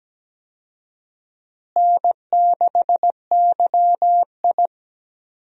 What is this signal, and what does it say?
Morse code, heard as a keyed tone, about 17 wpm, 700 Hz. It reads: N6YI
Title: USA Call Signs - 4 Characters - Repeated 2x - 17wpm